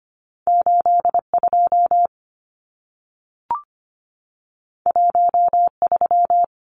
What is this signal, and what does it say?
Morse code, heard as a keyed tone, about 25 wpm, 700 Hz. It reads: 82 13
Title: Numbers - 2-Digit 25wpm